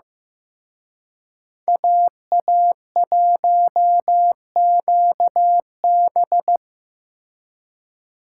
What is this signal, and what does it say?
Morse code, heard as a keyed tone, about 18 wpm, 700 Hz. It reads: AA1QB
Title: Speed Racing - USA 2x2 Call Signs 15wpm